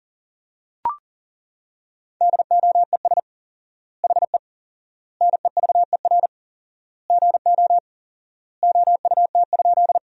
Morse code, heard as a keyed tone, about 40 wpm, 700 Hz. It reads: DOES HE NEVER GO OUT?
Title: 4x Word-Spacing - Sentences from Top 300 Words - 40wpm